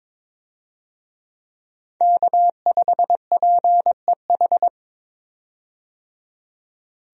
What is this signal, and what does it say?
Morse code, heard as a keyed tone, about 22 wpm, 700 Hz. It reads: K5PEH
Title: USA Call Signs - 5 Characters - 22wpm